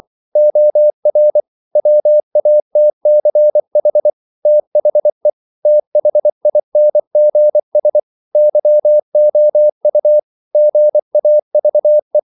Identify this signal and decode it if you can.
Morse code, heard as a keyed tone, about 24 wpm, 600 Hz. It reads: OR WATCH THE THINGS YOU GAVE
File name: CW / Practice / If24